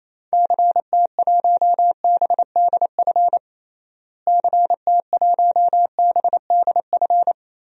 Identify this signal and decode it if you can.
Morse code, heard as a keyed tone, about 28 wpm, 700 Hz. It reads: CT1BDF CT1BDF
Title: DXCC-100 Call Signs - 6 Characters - Repeated 3x - 28wpm